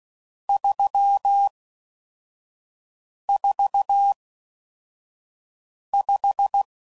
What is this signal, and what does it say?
Morse code, heard as a keyed tone, about 16 wpm, 800 Hz. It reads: 3 4 5